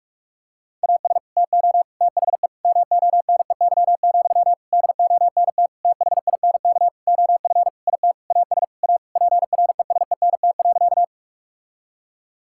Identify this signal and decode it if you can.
Morse code, heard as a keyed tone, about 41 wpm, 700 Hz. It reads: AS TO THE MONEY, DONT THINK OF IT AS A PRESENT.
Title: Sentences from Top 1000 Words - Review Part 5 - 45wpm